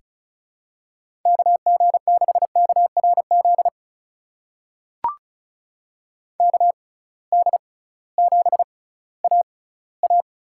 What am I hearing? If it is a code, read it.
Morse code, heard as a keyed tone, about 36 wpm, spaced as for 19 wpm, 700 Hz. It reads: KG6KRZ KD7AA
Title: USA Call Signs - 6 Characters - 6x Character Spacing - 35wpm